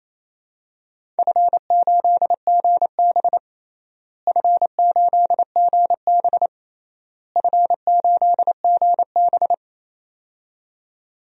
Morse code, heard as a keyed tone, about 28 wpm, 700 Hz. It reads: F8GB F8GB F8GB
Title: DXCC-100 Call Signs - 4 Characters - Repeated 3x - 28wpm